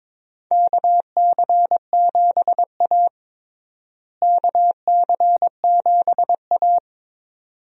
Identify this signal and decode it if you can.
Morse code, heard as a keyed tone, about 21 wpm, 700 Hz. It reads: KC7A KC7A
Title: USA Call Signs - 4 Characters - Repeated 2x - 22wpm